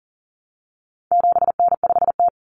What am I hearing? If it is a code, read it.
Morse code, heard as a keyed tone, about 40 wpm, 700 Hz. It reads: 7N5T
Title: Common Cut Numbers - 4-Digit 40wpm